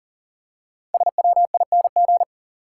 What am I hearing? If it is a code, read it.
Morse code, heard as a keyed tone, about 39 wpm, 700 Hz. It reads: SWING
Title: Five Letter Words - 40wpm